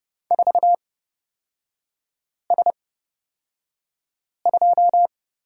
Morse code, heard as a keyed tone, about 31 wpm, spaced as for 5 wpm, 700 Hz. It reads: 4S2
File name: Lesson 38e - Letter q - Three Characters 30wpm